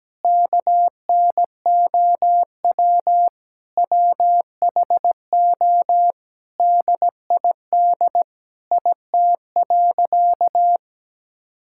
Morse code, heard as a keyed tone, about 17 wpm, 700 Hz. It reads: KNOW WHO DID IT.
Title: Sentences from Top 1000 Words - Review Part 6 - 17wpm